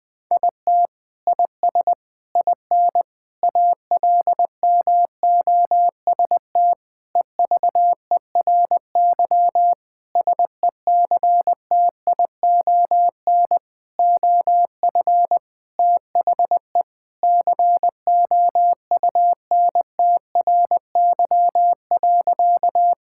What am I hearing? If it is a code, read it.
Morse code, heard as a keyed tone, about 21 wpm, 700 Hz. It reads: IT IS IN ALMOST EVERY SECTION OF THE COUNTRY.
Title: Sentences from Top 1000 Words - Review Part 5 - 20wpm